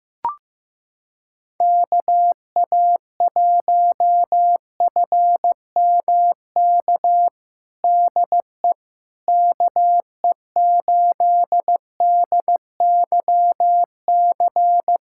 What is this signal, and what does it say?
Morse code, heard as a keyed tone, about 15 wpm, 700 Hz. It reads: KA1FMK DE KE8DYC
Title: USA Callsign DE Callsign - 6 Characters - 15wpm